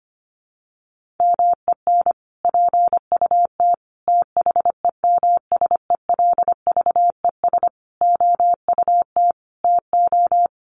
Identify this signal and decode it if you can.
Morse code, heard as a keyed tone, about 25 wpm, 700 Hz. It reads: MEN PUT THEMSELVES OUT TO
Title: Sentences from Top 1400 Words 25wpm